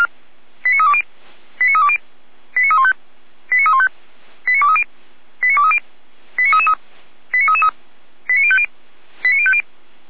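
Five-tone selective call (ZVEI), heard as quick runs of five tones, tones of about 70 ms. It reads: 89320 89320 89315 89315 89230 89230 89333 89333 89050 89050
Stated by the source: ZVEI five-tone call received by SDR, from the audio server of Feuerwehr Orlamünde fire brigade.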